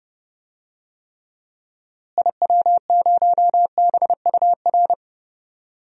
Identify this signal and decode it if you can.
Morse code, heard as a keyed tone, about 30 wpm, 700 Hz. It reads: IW0BUR
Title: DXCC-100 Call Signs - 6 Characters - Repeated 3x - 30wpm